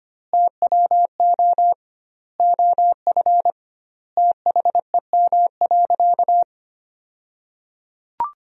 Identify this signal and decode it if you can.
Morse code, heard as a keyed tone, about 25 wpm, 700 Hz. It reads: TWO OF THEM.
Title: 2x Word-Spacing - Sentences from Top 100 Words - 25wpm